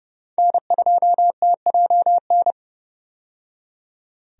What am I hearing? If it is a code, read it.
Morse code, heard as a keyed tone, about 30 wpm, 700 Hz. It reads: N2TJN
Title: USA Call Signs - 5 Characters - 30wpm